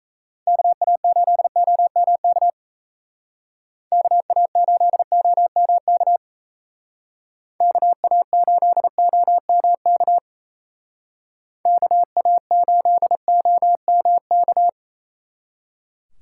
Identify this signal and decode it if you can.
Morse code, heard as a keyed tone, about 32 wpm, 700 Hz. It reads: KA8OMK KA8OMK KA8OMK KA8OMK
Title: Speed Racing - USA 2x3 Call Signs 28wpm